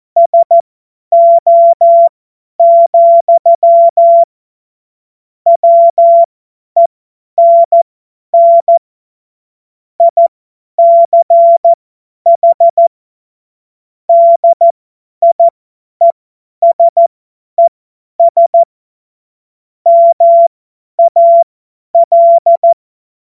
Morse code, heard as a keyed tone, about 14 wpm, spaced as for 10 wpm, 680 Hz. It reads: SO, WENN ICH DIESES MAL
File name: Morsekurs / Witze / Witz 02. November 2021, 50 F70